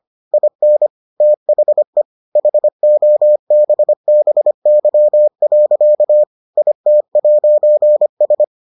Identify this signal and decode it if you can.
Morse code, heard as a keyed tone, about 25 wpm, 600 Hz. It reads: IN THE HOBBY. IT'S